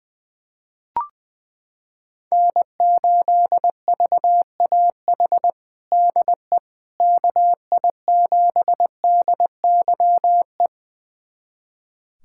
Morse code, heard as a keyed tone, about 20 wpm, 700 Hz. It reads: N8VAH DE KI7DYE
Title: USA Callsign DE Callsign 20wpm